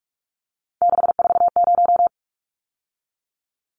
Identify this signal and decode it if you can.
Morse code, heard as a keyed tone, about 44 wpm, 700 Hz. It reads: 640
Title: Numbers - 3-Digit 45wpm